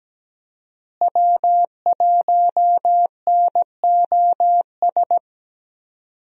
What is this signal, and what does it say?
Morse code, heard as a keyed tone, about 17 wpm, 700 Hz. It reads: W1NOS
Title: USA Call Signs - 5 Characters - Repeated 2x - 17wpm